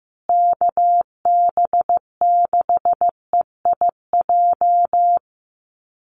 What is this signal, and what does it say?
Morse code, heard as a keyed tone, about 15 wpm, 700 Hz. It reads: KB6EIJ
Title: USA Call Signs Repeated 2x 15wpm